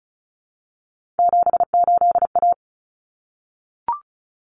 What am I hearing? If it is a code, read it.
Morse code, heard as a keyed tone, about 34 wpm, 700 Hz. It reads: 78A
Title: Common Cut Numbers - 3-Digit 35wpm